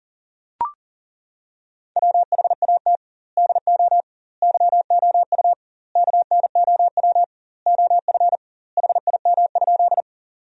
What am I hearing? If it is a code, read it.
Morse code, heard as a keyed tone, about 40 wpm, 700 Hz. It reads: WHAT DO YOU KNOW OF HIM?
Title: 2x Word-Spacing - Sentences from Top 100 Words - 40wpm